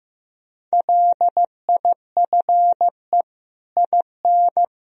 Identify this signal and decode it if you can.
Morse code, heard as a keyed tone, about 15 wpm, 700 Hz. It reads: LIFE IN
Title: Sentences from Top 1000 Words - Review Part 3 - 15wpm